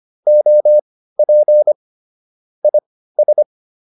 Morse code, heard as a keyed tone, about 25 wpm, spaced as for 15 wpm, 600 Hz. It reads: OP IS
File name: QSO107 15